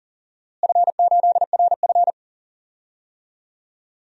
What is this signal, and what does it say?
Morse code, heard as a keyed tone, about 40 wpm, 700 Hz. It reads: F8RF
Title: DXCC-100 Call Signs - 4 Characters - Repeated 2x - 40wpm